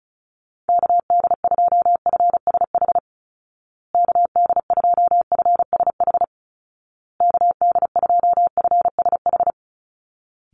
Morse code, heard as a keyed tone, about 34 wpm, 700 Hz. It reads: KD2FSH KD2FSH KD2FSH
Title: Call Signs Repeated 2x - 35wpm